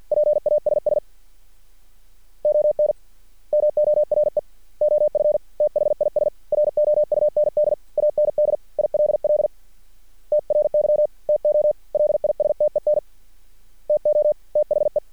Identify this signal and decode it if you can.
Morse code, heard as a keyed tone, about 50 wpm, 600 Hz. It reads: PASS ON MORE OF THIS ROUND AND ILL TRY TO LISTEN TO THE
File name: archived / CRSnet